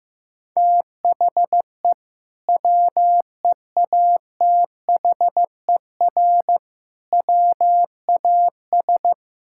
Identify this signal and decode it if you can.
Morse code, heard as a keyed tone, about 15 wpm, 700 Hz. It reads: THE WEATHER WAS
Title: Sentences from Top 1000 Words - Review Part 5 - 15wpm